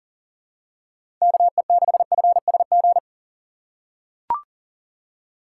Morse code, heard as a keyed tone, about 39 wpm, 700 Hz. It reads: KE6FSG
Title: USA Call Signs - 4x Character Spacing - 40wpm